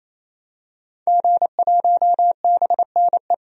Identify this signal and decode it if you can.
Morse code, heard as a keyed tone, about 28 wpm, 700 Hz. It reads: G1BNE
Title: DXCC-100 Call Signs - 5 Characters - 28wpm